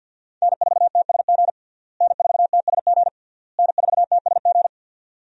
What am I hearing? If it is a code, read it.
Morse code, heard as a keyed tone, about 48 wpm, 700 Hz. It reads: N4TSG N4TSG N4TSG
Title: USA Call Signs - 5 Characters - Repeated 3x - 50wpm